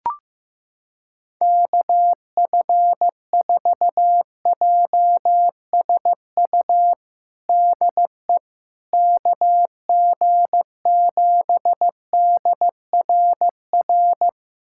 Morse code, heard as a keyed tone, about 15 wpm, 700 Hz. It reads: KF4JSU DE KG7DRR
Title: USA Callsign DE Callsign 15wpm